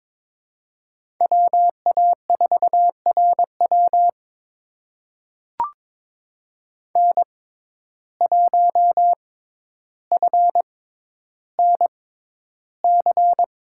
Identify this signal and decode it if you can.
Morse code, heard as a keyed tone, about 22 wpm, spaced as for 11 wpm, 700 Hz. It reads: WA4RW N1FNC
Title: USA Call Signs - 5 Characters - 6x Character Spacing - 22wpm